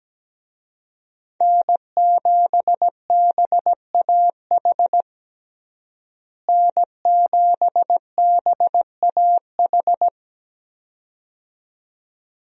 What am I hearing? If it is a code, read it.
Morse code, heard as a keyed tone, about 17 wpm, 700 Hz. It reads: N7BAH N7BAH
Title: USA Call Signs - 5 Characters - Repeated 2x - 17wpm